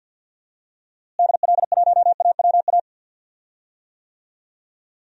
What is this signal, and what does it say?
Morse code, heard as a keyed tone, about 46 wpm, 700 Hz. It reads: DL1AWA
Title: DXCC-100 Call Signs - 6 Characters - Repeated 3x - 50wpm